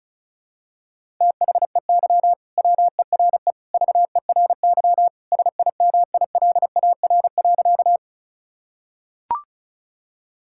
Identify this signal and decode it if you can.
Morse code, heard as a keyed tone, about 34 wpm, 700 Hz. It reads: THEY WERE VERY SIMILAR.
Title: Sentences from Top 1000 Words - Review Part 5 - 35wpm